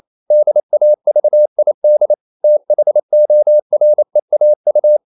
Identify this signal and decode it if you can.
Morse code, heard as a keyed tone, about 28 wpm, 600 Hz. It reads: DAVID THOREAU
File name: CW / Practice / EnglishQuotations0228